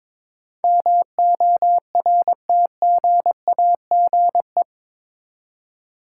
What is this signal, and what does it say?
Morse code, heard as a keyed tone, about 22 wpm, 700 Hz. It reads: MORTGAGE
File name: practice / 22wpm / Eight Letter Words - 22wpm